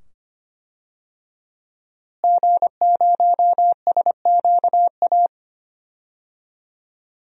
Morse code, heard as a keyed tone, about 25 wpm, 700 Hz. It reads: G0SQA
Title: DXCC-100 Call Signs - 5 Characters - Repeated 2x - 25wpm